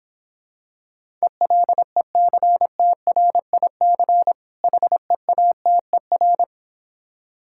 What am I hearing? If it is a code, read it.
Morse code, heard as a keyed tone, about 26 wpm, 700 Hz. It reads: ELECTRIC HEATER